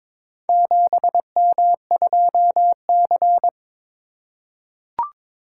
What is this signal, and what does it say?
Morse code, heard as a keyed tone, about 21 wpm, 700 Hz. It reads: 7M2C
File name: practice / 22wpm / Four Letter-Number Combinations - 22wpm